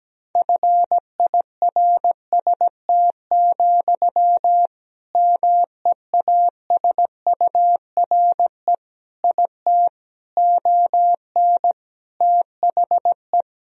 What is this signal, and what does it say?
Morse code, heard as a keyed tone, about 17 wpm, 700 Hz. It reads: FIRST, MEASURE IT ON THE
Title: Sentences from Top 1000 Words - Review Part 3 - 17wpm